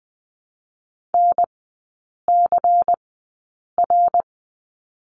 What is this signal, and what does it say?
Morse code, heard as a keyed tone, about 20 wpm, spaced as for 8 wpm, 700 Hz. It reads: NCR
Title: 100 Lesson 17e - Letter u - Three Characters 20wpm